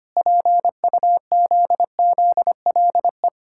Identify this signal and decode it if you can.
Morse code, heard as a keyed tone, about 25 wpm, 700 Hz. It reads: PUZZLE
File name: Easy Words - Speed-Racing with Hints - 25wpm